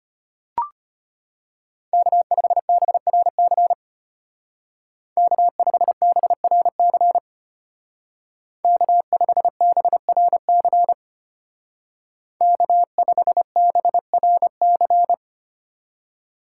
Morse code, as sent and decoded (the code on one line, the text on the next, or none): -.- ..... -... .-. -.-. / -.- ..... -... .-. -.-. / -.- ..... -... .-. -.-. / -.- ..... -... .-. -.-.
K5BRC K5BRC K5BRC K5BRC